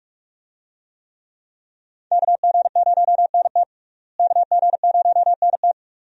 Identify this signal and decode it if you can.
Morse code, heard as a keyed tone, about 41 wpm, 700 Hz. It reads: KG0NT KG0NT
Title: USA Call Signs - 5 Characters - Repeated 2x - 45wpm